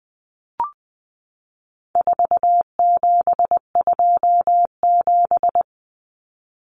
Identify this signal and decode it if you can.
Morse code, heard as a keyed tone, about 20 wpm, 700 Hz. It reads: 4727
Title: Numbers - 4-Digit 20wpm